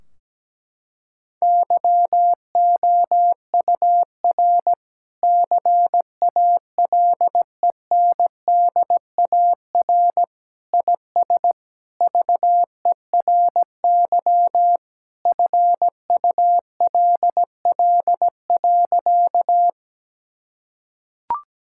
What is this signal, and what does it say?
Morse code, heard as a keyed tone, about 17 wpm, 700 Hz. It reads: YOUR CALENDAR IS VERY FULL.
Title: Sentences from Top 1600 Words 17wpm